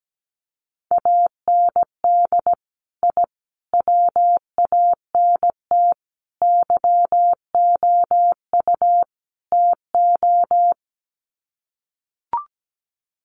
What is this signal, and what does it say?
Morse code, heard as a keyed tone, about 17 wpm, 700 Hz. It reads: AND I WANT YOU TO
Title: Sets of 5 Words Encore 17wpm